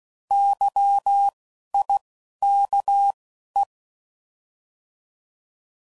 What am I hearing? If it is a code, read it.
Morse code, heard as a keyed tone, about 16 wpm, spaced as for 11 wpm, 800 Hz. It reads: YIKE
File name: Prefixes / By First Letter / Ys